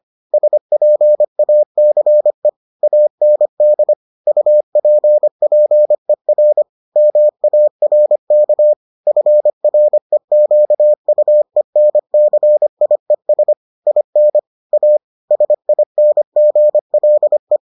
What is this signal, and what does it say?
Morse code, heard as a keyed tone, about 25 wpm, 600 Hz. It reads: SPACE AND UPPER MARK FREQUENCIES IN A SINGLE